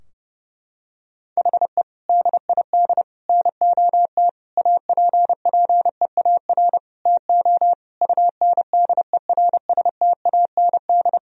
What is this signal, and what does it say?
Morse code, heard as a keyed tone, about 30 wpm, 700 Hz. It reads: HE DID NOT APPEAR TO UNDERSTAND
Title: Sentences from Top 1000 Words - Review Part 2 - 30wpm